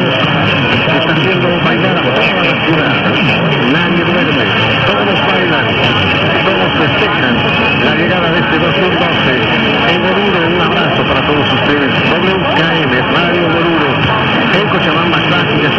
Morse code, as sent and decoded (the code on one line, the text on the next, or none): none